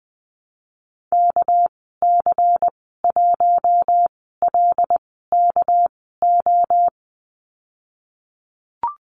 Morse code, heard as a keyed tone, about 20 wpm, spaced as for 14 wpm, 700 Hz. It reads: KC1LKO
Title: 213 Lesson 34g - Number 8 - Call Signs 20wpm